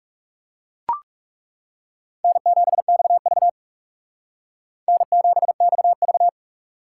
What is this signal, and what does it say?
Morse code, heard as a keyed tone, about 37 wpm, 700 Hz. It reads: N7XV N7XV
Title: Speed Racing - USA Call Signs 30wpm